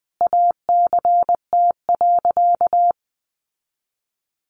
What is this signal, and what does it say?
Morse code, heard as a keyed tone, about 20 wpm, 700 Hz. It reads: ACT.
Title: Sentences from Top 900 Words 20wpm